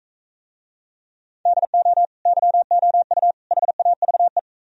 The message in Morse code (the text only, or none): DO YOU HAVE